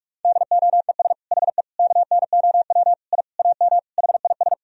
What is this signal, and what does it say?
Morse code, heard as a keyed tone, about 46 wpm, 700 Hz. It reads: DOES HE KNOW I AM HIS
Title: Sentences from Top 1000 Words - Review Part 2 - 45wpm